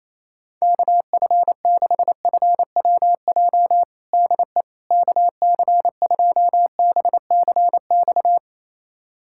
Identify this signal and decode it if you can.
Morse code, heard as a keyed tone, about 28 wpm, 700 Hz. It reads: KF6FWJ DE KC2BCX
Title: USA Callsign DE Callsign - 6 Characters - 28wpm